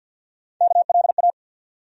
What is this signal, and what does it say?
Morse code, heard as a keyed tone, about 46 wpm, 700 Hz. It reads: KLA
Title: USA Call Sign Postfixes - 3 Letters 50wpm